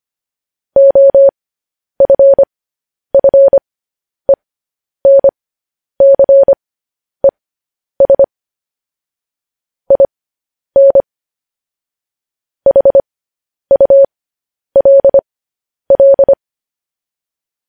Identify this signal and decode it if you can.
Morse code, heard as a keyed tone, about 25 wpm, spaced as for 10 wpm, 550 Hz. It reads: OFFENCES IN HULL
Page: news headlines in morse code at 10 wpm